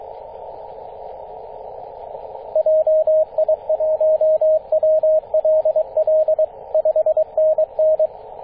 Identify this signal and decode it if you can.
Morse code, heard as a keyed tone, about 22 wpm, 630 Hz, with radio noise. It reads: JI1WLL 5NN
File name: 9m0o-24-891mhz-cw